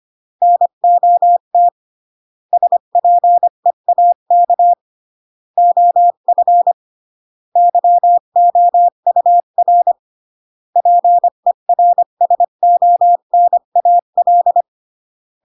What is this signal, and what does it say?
Morse code, heard as a keyed tone, about 25 wpm, 700 Hz. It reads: NOT SPEAK OF YOUR PERSONAL
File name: CW / jodorowsky / 82 commandments 25-20